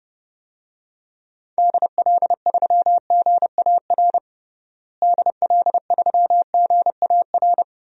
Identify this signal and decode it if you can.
Morse code, heard as a keyed tone, about 32 wpm, 700 Hz. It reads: DL3GAR DL3GAR
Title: DXCC-100 Call Signs - 6 Characters - Repeated 3x - 30wpm